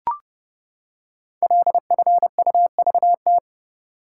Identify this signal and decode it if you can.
Morse code, heard as a keyed tone, about 30 wpm, 690 Hz. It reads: LFUVT
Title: Code Groups - 5 Letters - 30wpm